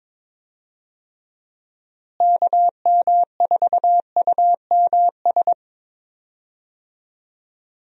Morse code, heard as a keyed tone, about 22 wpm, 700 Hz. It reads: KM4UMS
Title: USA Call Signs - 6 Characters - Repeated 2x - 22wpm